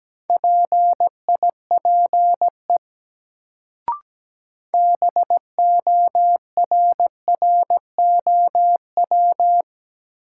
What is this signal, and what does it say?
Morse code, heard as a keyed tone, about 17 wpm, 700 Hz. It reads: PIPE BORROW
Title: Sentences from Top 1500 Words - New Word Drill - 17wpm